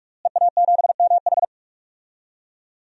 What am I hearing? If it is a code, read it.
Morse code, heard as a keyed tone, about 46 wpm, 700 Hz. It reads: EA7MH
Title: DXCC-100 Call Signs - 5 Characters - Repeated 2x - 45wpm